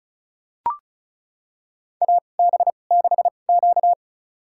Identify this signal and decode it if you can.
Morse code, heard as a keyed tone, about 36 wpm, spaced as for 25 wpm, 700 Hz. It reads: AB6Q
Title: USA Call Signs - 4 Characters - 2x Character Spacing - 35wpm